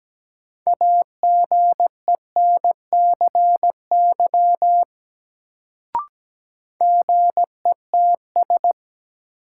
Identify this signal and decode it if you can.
Morse code, heard as a keyed tone, about 17 wpm, 700 Hz. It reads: AGENCY GETS
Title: Sentences from Top 1300 Words - New Word Drill - 17wpm